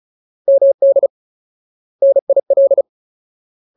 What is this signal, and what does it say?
Morse code, heard as a keyed tone, about 34 wpm, 550 Hz. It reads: MD NIL
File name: Ultimate Word List / 35WPM Ultimate